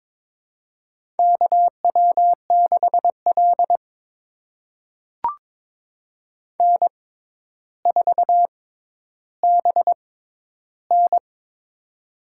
Morse code, heard as a keyed tone, about 23 wpm, spaced as for 11 wpm, 700 Hz. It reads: KW6L N4BN
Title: USA Call Signs - 4 Characters - 6x Character Spacing - 22wpm